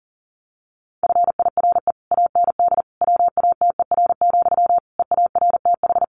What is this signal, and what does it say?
Morse code, heard as a keyed tone, about 40 wpm, 700 Hz. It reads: FIRE AND WATER, EARTH
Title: Sentences from Top 400 Words 40wpm